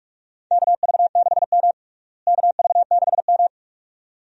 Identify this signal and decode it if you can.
Morse code, heard as a keyed tone, about 41 wpm, 700 Hz. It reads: KV6M KV6M
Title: USA Call Signs - 4 Characters - Repeated 2x - 45wpm